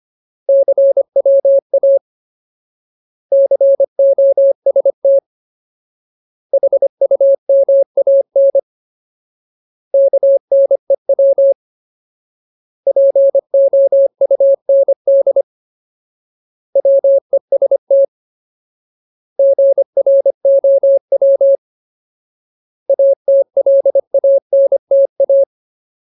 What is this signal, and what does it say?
Morse code, heard as a keyed tone, about 25 wpm, 550 Hz. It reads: CWA COST HUMAN KNEW POUND WEST GROW ATLANTA
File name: Ultimate Word List / 25WPM Ultimate